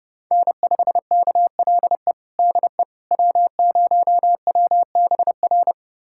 Morse code, heard as a keyed tone, about 30 wpm, 700 Hz. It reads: N5KLE DE W0WBR
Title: USA Callsign DE Callsign - 5 Characters - 30wpm